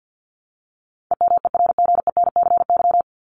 Morse code, heard as a keyed tone, about 48 wpm, 700 Hz. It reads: EMERGENCY
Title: Top 1501-1750 Words 50wpm